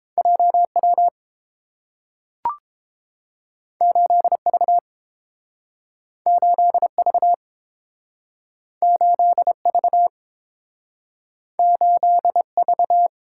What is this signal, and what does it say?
Morse code, heard as a keyed tone, about 30 wpm, 700 Hz. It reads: JW 8V 8V 8V 8V